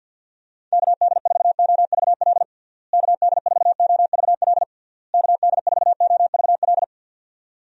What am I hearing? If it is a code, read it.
Morse code, heard as a keyed tone, about 50 wpm, 700 Hz. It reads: KD4OVL KD4OVL KD4OVL
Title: USA Call Signs - 6 Characters - Repeated 3x - 50wpm